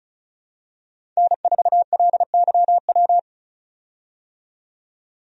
Morse code, heard as a keyed tone, about 34 wpm, 700 Hz. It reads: N4LYW
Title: USA Call Signs - 5 Characters - 35wpm